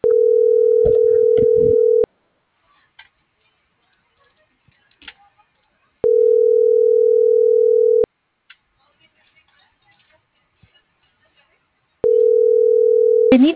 Ringback tone (North America), 440 + 480 Hz.